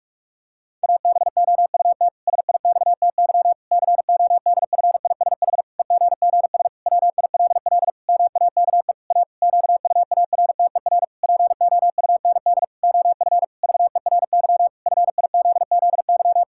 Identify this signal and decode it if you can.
Morse code, heard as a keyed tone, about 44 wpm, 700 Hz. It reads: ABOUT SIXTY CODFISH EGGS WILL MAKE A QUARTER POUND OF VERY FIZZY
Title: Pangram Challenge v2 - 45wpm